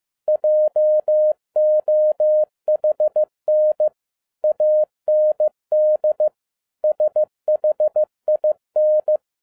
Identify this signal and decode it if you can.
Morse code, heard as a keyed tone, about 15 wpm, 610 Hz. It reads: JOHN AND SHIN
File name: 2011-07-mp3 / n47 JOHN K3TN 15wpm